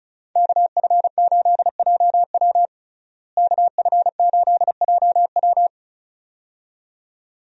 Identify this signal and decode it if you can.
Morse code, heard as a keyed tone, about 34 wpm, 700 Hz. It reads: KF8JW KF8JW
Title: USA Call Signs - 5 Characters - Repeated 2x - 35wpm